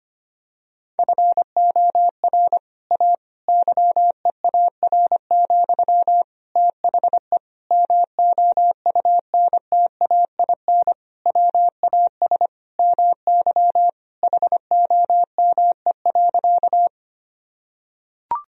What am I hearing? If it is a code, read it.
Morse code, heard as a keyed tone, about 25 wpm, 700 Hz. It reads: FOR A YEAR, THE MOUNTAIN WAS MY HOME.
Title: Sentences from Top 1000 Words - Review Part 1 - 25wpm